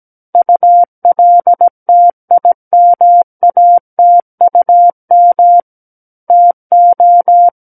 Morse code, heard as a keyed tone, about 17 wpm, 700 Hz. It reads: ULTIMATUM TO
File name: Book One / 08-Chapter 8 - 17 WPM